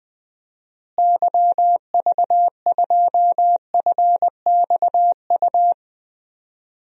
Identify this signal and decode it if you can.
Morse code, heard as a keyed tone, about 20 wpm, 700 Hz. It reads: YV2FXU
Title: DXCC-100 Call Signs - 6 Characters - Repeated 2x - 20wpm